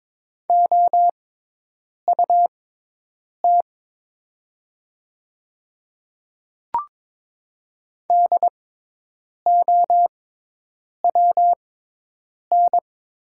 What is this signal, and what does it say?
Morse code, heard as a keyed tone, about 22 wpm, spaced as for 8 wpm, 700 Hz. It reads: OUT DOWN
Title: 6x Character Spacing - Top 100 Words - 22wpm